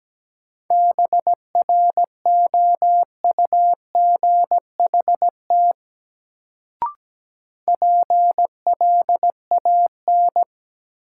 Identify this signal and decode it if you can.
Morse code, heard as a keyed tone, about 17 wpm, 700 Hz. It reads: BROUGHT PLAN
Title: Sentences from Top 500 Words - New Word Drill - 17wpm